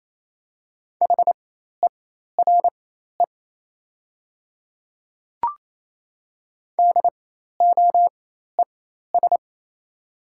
Morse code, heard as a keyed tone, about 28 wpm, spaced as for 13 wpm, 700 Hz. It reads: HERE DOES
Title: 4x Character Spacing - Top 200 Words - 28wpm